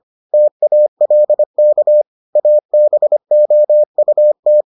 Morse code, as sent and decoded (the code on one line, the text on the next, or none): - .- .-.. -.- / .- -... --- ..- -
TALK ABOUT